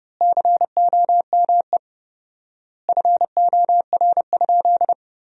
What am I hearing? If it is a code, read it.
Morse code, heard as a keyed tone, about 30 wpm, 700 Hz. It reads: COME FOR?
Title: 4x Word-Spacing - Sentences from Top 100 Words - 30wpm